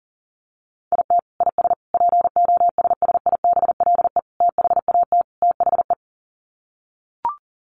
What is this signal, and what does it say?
Morse code, heard as a keyed tone, about 40 wpm, 700 Hz. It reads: IT IS POSSIBLE THAT THE
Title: Sets of 5 Words Encore2 40wpm